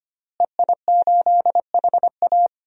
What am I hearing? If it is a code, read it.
Morse code, heard as a keyed tone, about 25 wpm, 700 Hz. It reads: EI8HA
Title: DXCC-100 Call Signs - 5 Characters - Repeated 3x - 25wpm